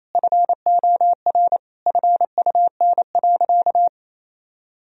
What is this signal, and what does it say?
Morse code, heard as a keyed tone, about 28 wpm, 700 Hz. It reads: FOR FUN.
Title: Sentences from Top 1000 Words - Review Part 3 - 28wpm